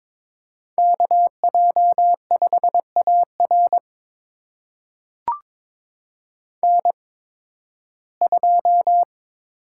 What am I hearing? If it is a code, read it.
Morse code, heard as a keyed tone, about 22 wpm, spaced as for 14 wpm, 700 Hz. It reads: KJ5AR N2
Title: USA Call Signs - 8x Character Spacing - 22wpm